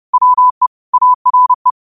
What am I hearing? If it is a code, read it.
Morse code, heard as a keyed tone, about 30 wpm, 1000 Hz. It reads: WE ARE